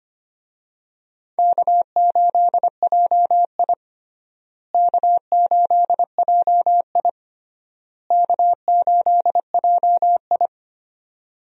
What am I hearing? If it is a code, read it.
Morse code, heard as a keyed tone, about 25 wpm, 700 Hz. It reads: K8JI K8JI K8JI
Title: USA Call Signs - 4 Characters - Repeated 3x - 25wpm